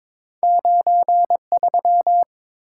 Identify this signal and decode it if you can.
Morse code, heard as a keyed tone, about 30 wpm, 700 Hz. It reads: 9 3